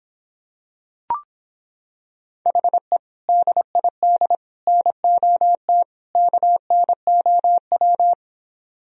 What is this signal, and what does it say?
Morse code, heard as a keyed tone, about 26 wpm, 700 Hz. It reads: HE DID NOT KNOW